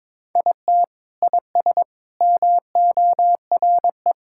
Morse code, heard as a keyed tone, about 22 wpm, 700 Hz. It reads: IT IS MORE
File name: practice / 22wpm / Sentences with 3-Word Phrases - 22wpm